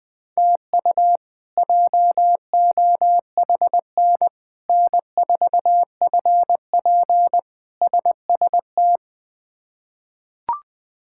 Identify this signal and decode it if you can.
Morse code, heard as a keyed tone, about 20 wpm, 700 Hz. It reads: TU JOHN N4FP SST
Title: K1USN SST QSO Practice - 20wpm